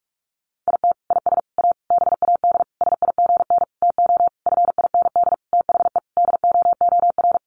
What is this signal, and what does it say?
Morse code, heard as a keyed tone, about 44 wpm, 700 Hz. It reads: IT IS A BAD SIGN TO FIND THE DOOR
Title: Sentences from Top 1000 Words 45wpm